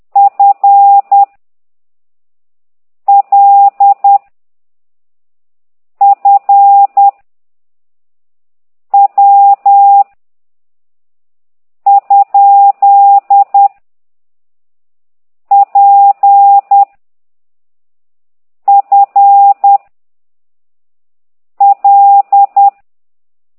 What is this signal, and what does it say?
Morse code, heard as a keyed tone, about 10 wpm, 800 Hz. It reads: F L F W ? P F L
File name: ct2idw / exerc9